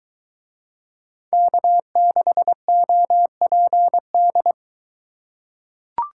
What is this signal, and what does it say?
Morse code, heard as a keyed tone, about 22 wpm, 700 Hz. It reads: K6OPD